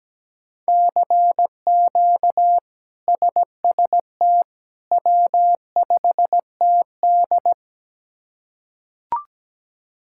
Morse code, heard as a keyed tone, about 17 wpm, 700 Hz. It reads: CQ SST W5TD
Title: K1USN SST QSO Practice - 17wpm